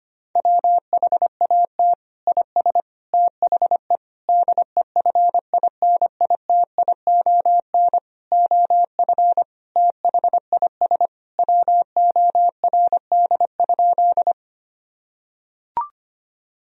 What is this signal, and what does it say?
Morse code, heard as a keyed tone, about 25 wpm, 700 Hz. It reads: WHAT IS THE DEFINITION OF THIS WORD?
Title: Sentences from Top 1000 Words - Review Part 4 - 25wpm